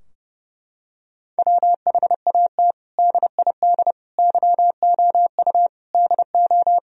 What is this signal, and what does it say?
Morse code, heard as a keyed tone, about 28 wpm, 700 Hz. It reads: WHAT DID YOU DO
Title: Sentences from Top 1000 Words - Review Part 4 - 30wpm